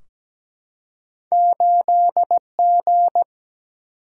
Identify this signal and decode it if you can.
Morse code, heard as a keyed tone, about 17 wpm, 700 Hz. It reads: 8G
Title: Single-Letter Single-Number 17wpm